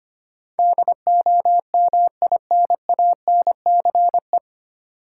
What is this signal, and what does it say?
Morse code, heard as a keyed tone, about 25 wpm, 700 Hz. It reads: DOMINANCE
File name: practice / 25wpm / 9 Letter Words - First 3 Letters Given - 25wpm